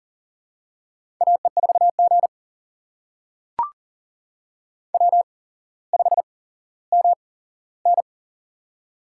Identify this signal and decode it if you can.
Morse code, heard as a keyed tone, about 42 wpm, spaced as for 17 wpm, 690 Hz. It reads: AE4G W5MN
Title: USA Call Signs - 4 Characters - 8x Character Spacing - 40wpm